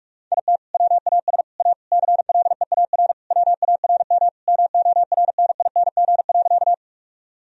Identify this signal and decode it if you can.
Morse code, heard as a keyed tone, about 46 wpm, 700 Hz. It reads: IT WAS A CLEAR WARM MORNING.
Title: Sentences from Top 1000 Words - Review Part 2 - 45wpm